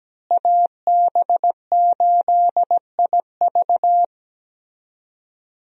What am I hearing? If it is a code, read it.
Morse code, heard as a keyed tone, about 17 wpm, 700 Hz. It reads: AB8IV
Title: USA Call Signs - 5 Characters - Repeated 3x - 17wpm